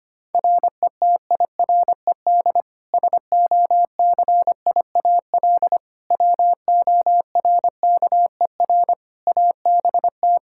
Morse code, heard as a keyed tone, about 25 wpm, 700 Hz. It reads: RETIRED SOCIAL WORKER ABT
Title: K1USN SST QSO Practice 25wpm